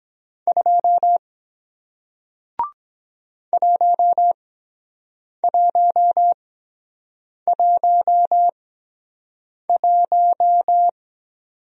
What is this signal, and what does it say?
Morse code, heard as a keyed tone, about 23 wpm, spaced as for 7 wpm, 700 Hz. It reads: 2 1111